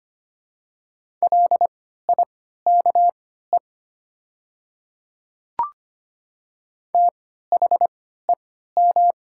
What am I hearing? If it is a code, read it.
Morse code, heard as a keyed tone, about 25 wpm, spaced as for 14 wpm, 700 Hz. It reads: LIKE THEM
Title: 3x Character Spacing - Top 100 Words - 25wpm